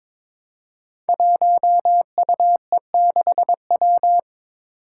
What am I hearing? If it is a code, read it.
Morse code, heard as a keyed tone, about 22 wpm, 700 Hz. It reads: 1UE6W
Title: Five Challenging Letter-Number Combinations 22wpm